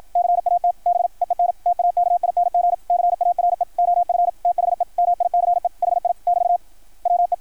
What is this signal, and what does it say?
Morse code, heard as a keyed tone, about 53 wpm, 690 Hz. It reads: CAT BEEN TAKING CARE OF THE MICE HI <BT> WE
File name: archived / CRSnet